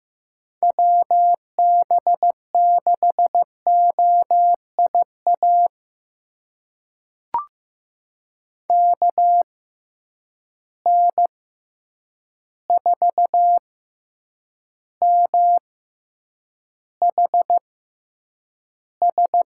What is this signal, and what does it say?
Morse code, heard as a keyed tone, about 15 wpm, spaced as for 8 wpm, 700 Hz. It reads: WB6OIA KN4MHS
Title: USA Call Signs - 6 Characters - 6x Character Spacing - 15wpm